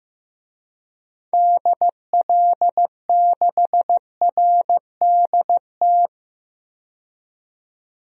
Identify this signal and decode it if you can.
Morse code, heard as a keyed tone, about 15 wpm, 700 Hz. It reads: DL6RDT
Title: DXCC-100 Call Signs - 6 Characters - 15wpm